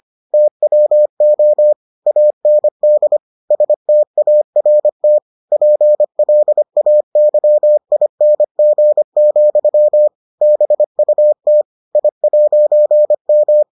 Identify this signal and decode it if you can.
Morse code, heard as a keyed tone, about 25 wpm, 600 Hz. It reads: TWO AND START PLAYING, BUT I'M